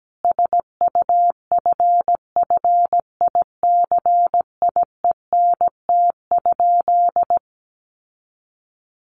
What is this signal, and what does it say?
Morse code, heard as a keyed tone, about 17 wpm, 700 Hz. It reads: SUFFICIENT?
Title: Sentences from Top 1400 Words 17wpm